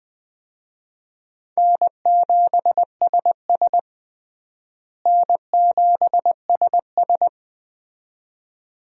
Morse code, heard as a keyed tone, about 20 wpm, 700 Hz. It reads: N7SS N7SS
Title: USA Call Signs - 4 Characters - Repeated 2x - 20wpm